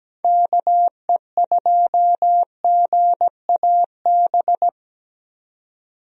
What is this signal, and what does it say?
Morse code, heard as a keyed tone, about 17 wpm, 700 Hz. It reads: KE2GAB
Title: USA Call Signs - 6 Characters - 17wpm